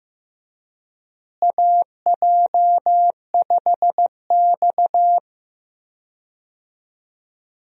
Morse code, heard as a keyed tone, about 15 wpm, 700 Hz. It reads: AJ5X
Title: USA Call Signs - 4 Characters - Repeated 2x - 15wpm